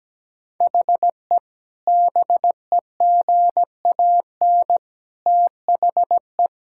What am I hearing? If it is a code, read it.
Morse code, heard as a keyed tone, about 17 wpm, 700 Hz. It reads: HE BEGAN THE
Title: Sentences from Top 1000 Words - Review Part 5 - 17wpm